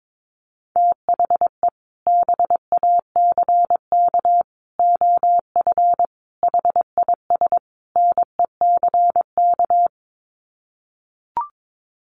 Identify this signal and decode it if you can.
Morse code, heard as a keyed tone, about 22 wpm, 700 Hz. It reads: THE BACK OF HIS NECK
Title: Sets of 5 Words Encore 22wpm